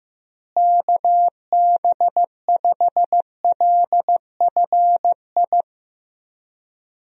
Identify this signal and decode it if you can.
Morse code, heard as a keyed tone, about 16 wpm, 700 Hz. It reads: KB5LFI
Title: USA Call Signs - 6 Characters - Repeated 3x - 15wpm